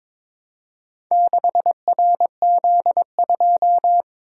Morse code, heard as a keyed tone, about 22 wpm, 700 Hz. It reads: 6RZ2
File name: practice / 22wpm / Four Letter-Number Combinations - 22wpm